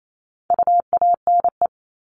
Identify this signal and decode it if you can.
Morse code, heard as a keyed tone, about 28 wpm, 700 Hz. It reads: UANE
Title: All Cut Numbers - 4-Digit 28wpm